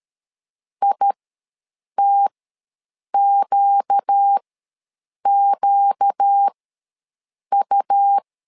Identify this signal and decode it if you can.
Morse code, heard as a keyed tone, about 13 wpm, spaced as for 7 wpm, 800 Hz. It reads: ITQQU
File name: LEZIONI / 4 Q ws 12wpm 1912